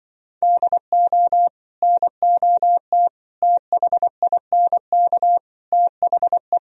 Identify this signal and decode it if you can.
Morse code, heard as a keyed tone, about 25 wpm, 700 Hz. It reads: DO NOT THINK THE